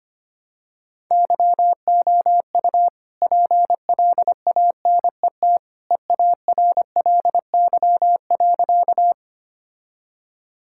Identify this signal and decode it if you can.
Morse code, heard as a keyed tone, about 25 wpm, 700 Hz. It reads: YOU PLANET EARLY.
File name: practice / 25wpm / All-Age Appropriate Jokes - 25wpm